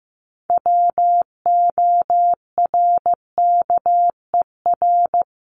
Morse code, heard as a keyed tone, about 15 wpm, 700 Hz. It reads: WORKER
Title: Top 1751-2000 Words 15wpm